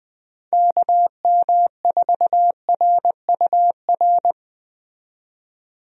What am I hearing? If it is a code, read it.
Morse code, heard as a keyed tone, about 20 wpm, 700 Hz. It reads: KM4RUR
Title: USA Call Signs - 6 Characters - 20wpm